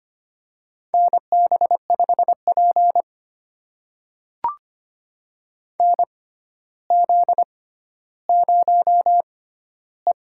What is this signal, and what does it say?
Morse code, heard as a keyed tone, about 25 wpm, spaced as for 13 wpm, 700 Hz. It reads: NB5P NZ0E
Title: USA Call Signs - 4 Characters - 6x Character Spacing - 25wpm